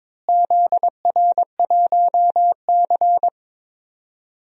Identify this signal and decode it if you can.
Morse code, heard as a keyed tone, about 22 wpm, 700 Hz. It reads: ZR1C
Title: DXCC-100 Call Signs - 4 Characters - Repeated 3x - 22wpm